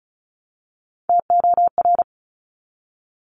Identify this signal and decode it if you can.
Morse code, heard as a keyed tone, about 36 wpm, 700 Hz. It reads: TOR
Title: Top Three Letter Combinations in Words 35wpm